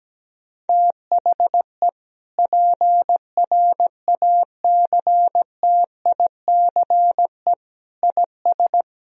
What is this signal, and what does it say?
Morse code, heard as a keyed tone, about 17 wpm, 700 Hz. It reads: THE PRACTICE IS
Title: Sentences from Top 1000 Words - Review Part 4 - 17wpm